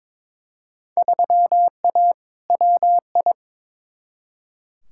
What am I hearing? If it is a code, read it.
Morse code, heard as a keyed tone, about 22 wpm, 700 Hz. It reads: 3A WI
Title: Field Day Exchanges - 22wpm